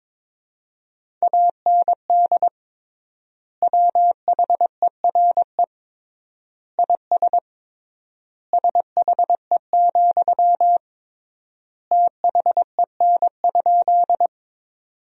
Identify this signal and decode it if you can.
Morse code, heard as a keyed tone, about 21 wpm, 700 Hz. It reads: AND WHERE IS SHE, THEN?
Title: 3x Word-Spacing - Sentences from Top 200 Words - 22wpm